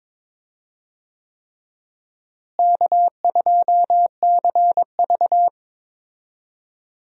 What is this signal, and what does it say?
Morse code, heard as a keyed tone, about 22 wpm, 700 Hz. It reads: K2CV
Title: USA Call Signs - 4 Characters - Repeated 3x - 22wpm